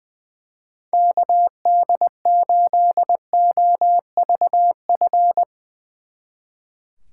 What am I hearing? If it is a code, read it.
Morse code, heard as a keyed tone, about 22 wpm, 700 Hz. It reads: KD8OVF
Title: Speed Racing - USA 2x3 Call Signs 20wpm